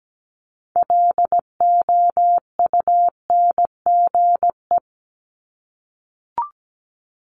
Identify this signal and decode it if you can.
Morse code, heard as a keyed tone, about 17 wpm, 700 Hz. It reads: LOUNGE
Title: Multi-letter Phonemes 17wpm